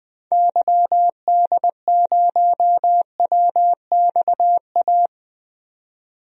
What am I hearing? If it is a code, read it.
Morse code, heard as a keyed tone, about 20 wpm, 700 Hz. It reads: YD0WXA
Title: DXCC-100 Call Signs - 6 Characters - 20wpm